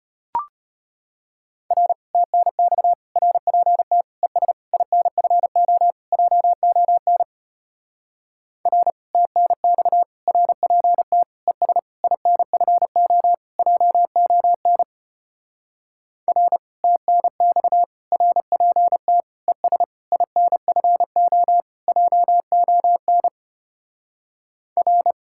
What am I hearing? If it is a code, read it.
Morse code, heard as a keyed tone, about 33 wpm, 700 Hz. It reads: R TNX RPT ES INFO JON R TNX RPT ES INFO JON R TNX RPT ES INFO JON R